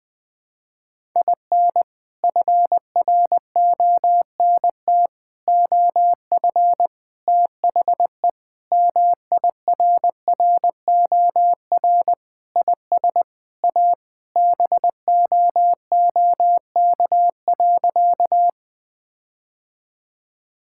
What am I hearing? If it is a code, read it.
Morse code, heard as a keyed tone, about 20 wpm, 700 Hz. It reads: IN FRONT OF THE MIRROR IS A BOOK.
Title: Sentences with 4-Word Phrases - Spoken Beginning - 20wpm